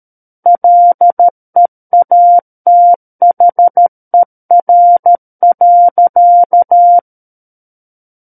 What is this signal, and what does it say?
Morse code, heard as a keyed tone, about 13 wpm, 700 Hz. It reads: LEATHER.
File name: Book One / 04-Chapter 4 - 13 WPM